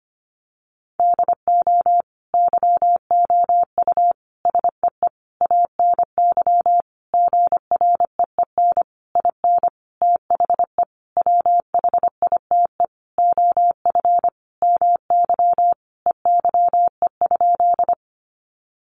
Morse code, heard as a keyed tone, about 25 wpm, 700 Hz. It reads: DO YOU SEE ANY GREEN IN THE WHITE OF MY EYE?
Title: Sentences from Top 500 Words 25wpm